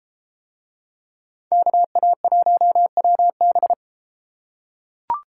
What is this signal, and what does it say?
Morse code, heard as a keyed tone, about 32 wpm, 700 Hz. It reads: KA1WB